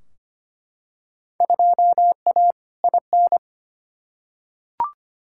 Morse code, heard as a keyed tone, about 25 wpm, 700 Hz. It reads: 2A IN
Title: Field Day Exchanges - 25wpm